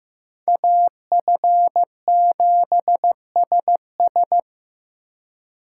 Morse code, heard as a keyed tone, about 18 wpm, 700 Hz. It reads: AF7SS
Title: Speed Racing - USA 2x2 Call Signs 15wpm